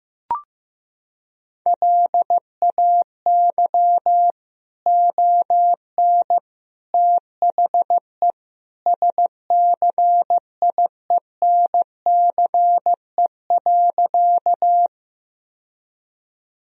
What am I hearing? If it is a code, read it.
Morse code, heard as a keyed tone, about 15 wpm, 700 Hz. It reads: LAY ON THE SCIENCE.
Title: Sentences from Top 1000 Words - Review Part 2 - 15wpm